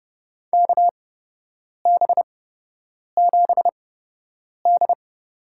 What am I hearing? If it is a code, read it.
Morse code, heard as a keyed tone, about 30 wpm, spaced as for 8 wpm, 700 Hz. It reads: KB7D
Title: USA Call Signs - 6 Characters - 8x Character Spacing - 30wpm